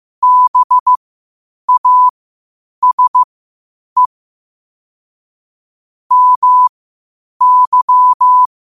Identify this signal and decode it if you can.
Morse code, heard as a keyed tone, about 15 wpm, spaced as for 8 wpm, 1000 Hz. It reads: BASE MY
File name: 8wpm